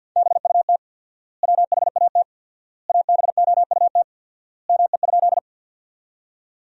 Morse code, heard as a keyed tone, about 50 wpm, 700 Hz. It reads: BUT WHAT ABOUT ME?
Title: 4x Word-Spacing - Sentences from Top 200 Words - 50wpm